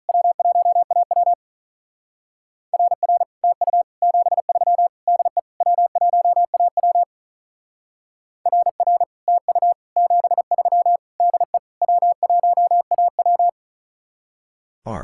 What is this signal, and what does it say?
Morse code, heard as a keyed tone, about 44 wpm, 700 Hz. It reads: W1AW RR TU 73 DE W1AW RR TU 73 DE W1AW